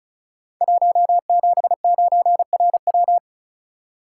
Morse code, heard as a keyed tone, about 36 wpm, 700 Hz. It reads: 179RW
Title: Code Groups - 5 Letter:Numbers - 35wpm